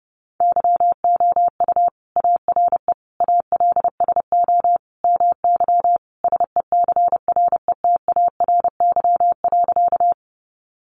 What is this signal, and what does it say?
Morse code, heard as a keyed tone, about 30 wpm, 700 Hz. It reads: YOU ARE ALSO MY SECRETARY.
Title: Sentences from Top 1300 Words 30wpm